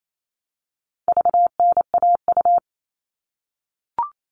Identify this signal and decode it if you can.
Morse code, heard as a keyed tone, about 28 wpm, 700 Hz. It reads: VNAU
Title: All Cut Numbers - 4-Digit 28wpm